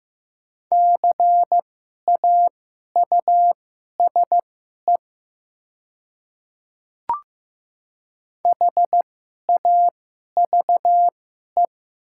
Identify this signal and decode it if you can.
Morse code, heard as a keyed tone, about 15 wpm, spaced as for 11 wpm, 700 Hz. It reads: CAUSE HAVE
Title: 2x Character Spacing - Top 500 Words - 15wpm